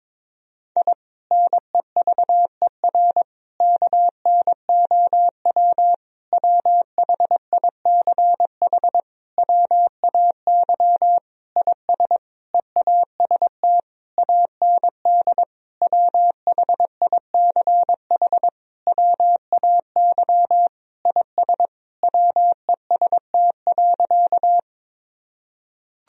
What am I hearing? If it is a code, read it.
Morse code, heard as a keyed tone, about 22 wpm, 700 Hz. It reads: I NEVER KNOW WHICH WAY IS EAST AND WHICH WAY IS WEST.
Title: Sentences from Top 1000 Words - Review Part 3 - 22wpm